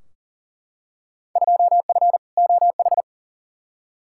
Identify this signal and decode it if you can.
Morse code, heard as a keyed tone, about 40 wpm, 700 Hz. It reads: 2F OH
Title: Field Day Exchanges - 40wpm